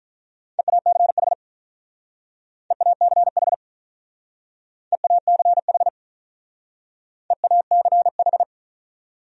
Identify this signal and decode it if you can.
Morse code, heard as a keyed tone, about 45 wpm, 700 Hz. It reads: EACH EACH EACH EACH